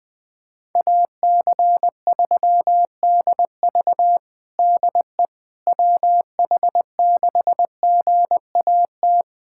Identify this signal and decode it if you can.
Morse code, heard as a keyed tone, about 20 wpm, 700 Hz. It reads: AC3DV DE WH6GAT
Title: USA Callsign DE Callsign 20wpm